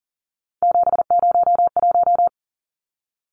Morse code, heard as a keyed tone, about 40 wpm, 700 Hz. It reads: 701
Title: Numbers - 3-Digit 40wpm